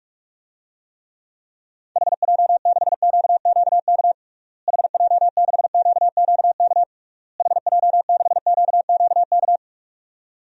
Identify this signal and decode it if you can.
Morse code, heard as a keyed tone, about 41 wpm, 700 Hz. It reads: HJ6QQK HJ6QQK HJ6QQK
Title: DXCC-100 Call Signs - 6 Characters - Repeated 3x - 45wpm